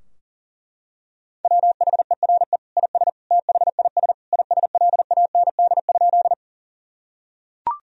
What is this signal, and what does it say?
Morse code, heard as a keyed tone, about 40 wpm, 700 Hz. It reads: WHERE IS THIS ISLAND?
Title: Sentences from Top 1000 Words - Review Part 3 - 40wpm